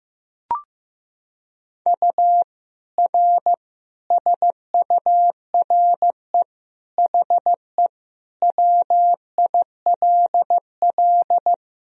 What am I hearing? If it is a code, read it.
Morse code, heard as a keyed tone, about 16 wpm, 700 Hz. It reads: U R SURE HE WILL
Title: Sentences from Top 300 Words and Abbreviations 15wpm